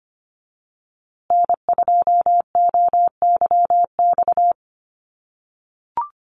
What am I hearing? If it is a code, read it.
Morse code, heard as a keyed tone, about 25 wpm, 700 Hz. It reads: N2OYX
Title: USA Call Signs Repeated 3x 25wpm